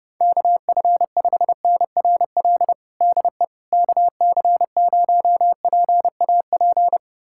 Morse code, heard as a keyed tone, about 32 wpm, 700 Hz. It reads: KF5NRL DE KC0PAP
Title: USA Callsign DE Callsign 30wpm